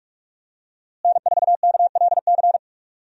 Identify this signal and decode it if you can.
Morse code, heard as a keyed tone, about 46 wpm, 700 Hz. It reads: N4KLC
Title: USA Call Signs - 5 Characters - Repeated 2x - 45wpm